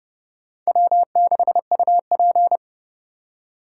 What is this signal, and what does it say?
Morse code, heard as a keyed tone, about 27 wpm, 700 Hz. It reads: W6UP